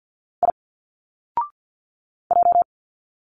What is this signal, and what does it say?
Morse code, heard as a keyed tone, about 50 wpm, 700 Hz. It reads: I J
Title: Single Letters 50wpm